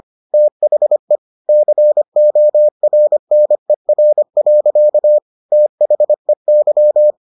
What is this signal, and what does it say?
Morse code, heard as a keyed tone, about 25 wpm, 600 Hz. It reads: THE CORNER. THEY